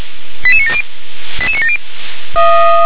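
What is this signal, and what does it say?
Five-tone selective call (ZVEI), heard as quick runs of five tones, tones of about 70 ms. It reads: 88970 88970
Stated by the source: ZVEI five-tone selective call received on an SDR.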